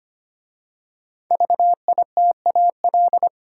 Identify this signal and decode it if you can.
Morse code, heard as a keyed tone, about 25 wpm, 700 Hz. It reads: VITAL
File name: 5 Letter Words - First 3 Letters Given - 25wpm